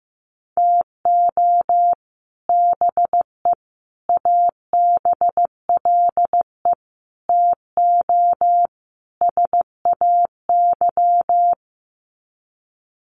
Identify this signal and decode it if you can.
Morse code, heard as a keyed tone, about 15 wpm, 700 Hz. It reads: TO BE ABLE TO SAY
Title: Sets of 5 Words Encore2 15wpm